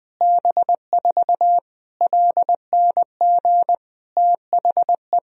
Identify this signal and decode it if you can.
Morse code, heard as a keyed tone, about 20 wpm, 700 Hz. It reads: B4 LNG THE
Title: Sentences from Top 400 Words and Abbreviations 20wpm